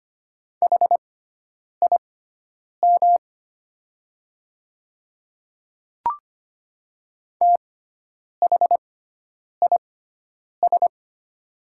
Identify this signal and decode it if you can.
Morse code, heard as a keyed tone, about 25 wpm, spaced as for 9 wpm, 700 Hz. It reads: HIM THIS
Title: 6x Character Spacing - Top 100 Words - 25wpm